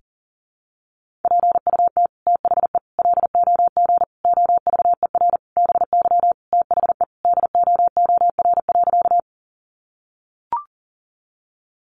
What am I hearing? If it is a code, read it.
Morse code, heard as a keyed tone, about 40 wpm, 700 Hz. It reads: PUT THE LOG OVER BY THE DOOR.
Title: Sentences from Top 1400 Words 40wpm